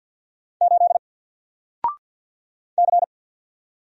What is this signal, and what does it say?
Morse code, heard as a keyed tone, about 48 wpm, 700 Hz. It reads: 8 C
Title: Single Character (Letter:Number:Punctuation:Prosign) Alt 50wpm